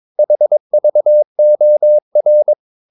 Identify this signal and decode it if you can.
Morse code, heard as a keyed tone, about 22 wpm, 600 Hz. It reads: HVOR
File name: Ferie QRS 21wpm